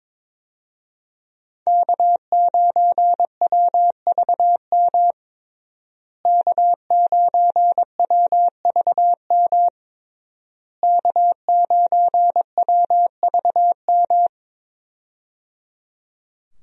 Morse code, heard as a keyed tone, about 22 wpm, 700 Hz. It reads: K9WVM K9WVM K9WVM
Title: USA Call Signs - 5 Characters - Repeated 3x - 22wpm